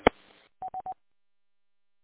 There is no code.